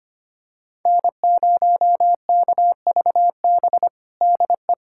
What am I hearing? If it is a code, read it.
Morse code, heard as a keyed tone, about 25 wpm, 700 Hz. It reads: N0KVB DE
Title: USA Callsign DE Callsign - 5 Characters - 25wpm